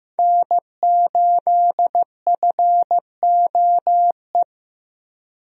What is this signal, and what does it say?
Morse code, heard as a keyed tone, about 15 wpm, 700 Hz. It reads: N8FOE
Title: USA Call Signs - 5 Characters - Repeated 2x - 15wpm